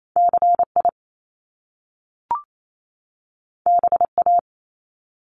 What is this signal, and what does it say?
Morse code, heard as a keyed tone, about 28 wpm, 700 Hz. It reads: CI BA
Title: Top Two Letter Combinations in Words 28wpm